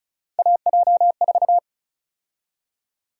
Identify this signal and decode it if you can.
Morse code, heard as a keyed tone, about 36 wpm, 700 Hz. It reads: AJ4
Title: USA Call Sign Prefixes - 35wpm